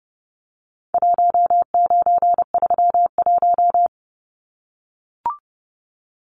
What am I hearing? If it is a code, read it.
Morse code, heard as a keyed tone, about 30 wpm, 700 Hz. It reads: 1931
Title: Numbers - 4-Digit 30wpm